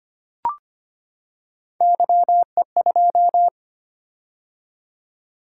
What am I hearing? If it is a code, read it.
Morse code, heard as a keyed tone, about 25 wpm, 700 Hz. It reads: YE2
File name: Three Letter-Number Combinations - 25wpm